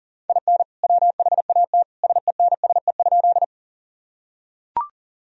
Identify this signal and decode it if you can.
Morse code, heard as a keyed tone, about 40 wpm, 700 Hz. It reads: IN WHAT SENSE?
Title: Sentences from Top 1000 Words - Review Part 4 - 40wpm